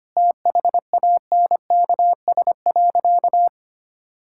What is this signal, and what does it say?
Morse code, heard as a keyed tone, about 25 wpm, 700 Hz. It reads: THANKS.
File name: practice / 25wpm / Sentences with 4-Word Phrases - 25wpm